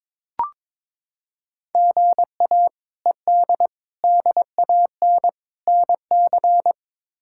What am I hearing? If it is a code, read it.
Morse code, heard as a keyed tone, about 22 wpm, 700 Hz. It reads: GA ED DAN NC
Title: K1USN SST QSO Practice - 22wpm